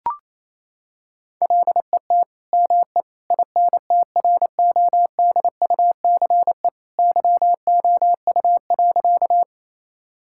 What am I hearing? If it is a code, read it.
Morse code, heard as a keyed tone, about 28 wpm, 700 Hz. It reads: LET ME INTRODUCE YOU.
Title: Sentences from Top 1600 Words 28wpm